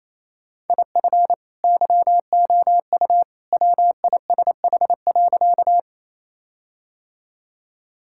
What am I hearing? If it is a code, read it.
Morse code, heard as a keyed tone, about 27 wpm, 700 Hz. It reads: IF YOU WISH.
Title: Sentences from Top 1600 Words 28wpm